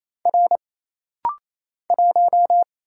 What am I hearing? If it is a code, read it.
Morse code, heard as a keyed tone, about 27 wpm, 700 Hz. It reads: R 1
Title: Single Letter-Number Alt - Rapid-Fire 28wpm